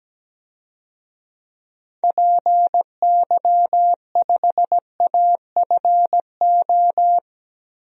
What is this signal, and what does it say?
Morse code, heard as a keyed tone, about 17 wpm, 700 Hz. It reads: PY5AFO
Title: DXCC-100 Call Signs - 6 Characters - 17wpm